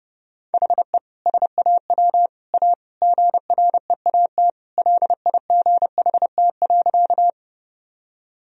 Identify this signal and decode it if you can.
Morse code, heard as a keyed tone, about 30 wpm, 700 Hz. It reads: HE SAW A GREAT LIGHT.
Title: Sentences from Top 1000 Words - Review Part 1 - 30wpm